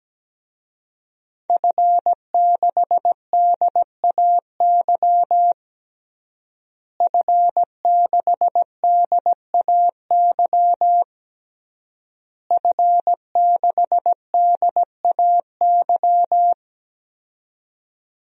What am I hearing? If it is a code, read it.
Morse code, heard as a keyed tone, about 17 wpm, 700 Hz. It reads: F6DAY F6DAY F6DAY
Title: DXCC-100 Call Signs - 5 Characters - Repeated 3x - 17wpm